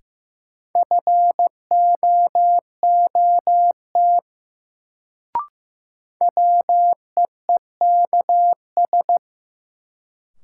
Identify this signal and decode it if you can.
Morse code, heard as a keyed tone, about 15 wpm, 700 Hz. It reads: FOOT WEEKS
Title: Sets of 5 Words - New Word Drill - 15wpm